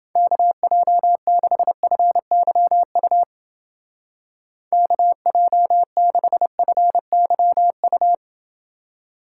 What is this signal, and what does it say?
Morse code, heard as a keyed tone, about 29 wpm, 700 Hz. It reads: KJ6FYU KJ6FYU